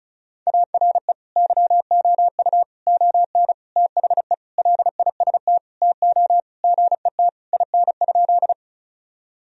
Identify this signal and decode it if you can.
Morse code, heard as a keyed tone, about 36 wpm, 700 Hz. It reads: ARE YOU ON THE LIST TO GET IN?
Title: Sentences from Top 1000 Words - Review Part 2 - 35wpm